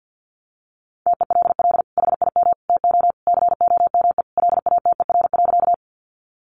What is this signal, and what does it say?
Morse code, heard as a keyed tone, about 50 wpm, 700 Hz. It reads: TELL HIM TO COME LATER.
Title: Sentences from Top 700 Words 50wpm